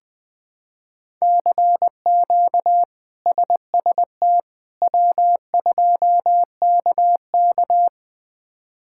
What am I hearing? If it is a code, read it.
Morse code, heard as a keyed tone, about 20 wpm, 700 Hz. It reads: CQ SST W2KK
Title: K1USN SST QSO Practice - 20wpm